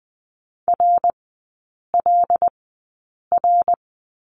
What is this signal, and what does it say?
Morse code, heard as a keyed tone, about 20 wpm, spaced as for 8 wpm, 700 Hz. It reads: RLR
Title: 074 Lesson 13e - Letter L - Three Characters 20wpm